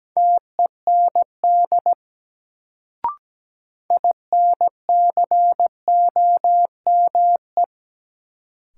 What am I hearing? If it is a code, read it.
Morse code, heard as a keyed tone, about 17 wpm, 700 Hz. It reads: TEND INCOME
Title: Sentences from Top 1000 Words - Review Part 5 - New Word Drill - 17wpm